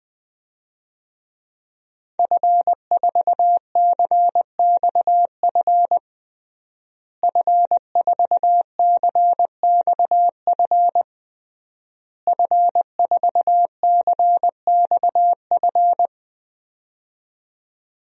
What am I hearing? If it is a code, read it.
Morse code, heard as a keyed tone, about 20 wpm, 700 Hz. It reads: F4CXF F4CXF F4CXF
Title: DXCC-100 Call Signs - 5 Characters - Repeated 3x - 20wpm